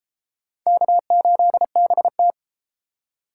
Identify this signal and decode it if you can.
Morse code, heard as a keyed tone, about 32 wpm, 700 Hz. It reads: K8BT